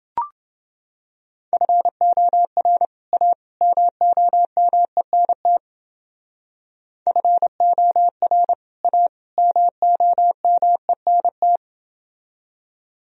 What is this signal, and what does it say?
Morse code, heard as a keyed tone, about 28 wpm, 700 Hz. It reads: FOR A MOMENT FOR A MOMENT